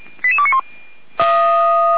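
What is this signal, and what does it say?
Five-tone selective call (ZVEI), heard as quick runs of five tones, tones of about 70 ms. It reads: 88291
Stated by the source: Five-tone ZVEI selective call 88333 received on an SDR.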